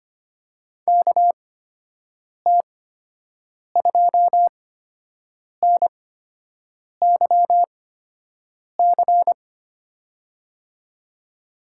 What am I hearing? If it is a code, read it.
Morse code, heard as a keyed tone, about 25 wpm, spaced as for 7 wpm, 700 Hz. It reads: KT2NYC
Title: USA Call Signs - 6 Characters - 8x Character Spacing - 25wpm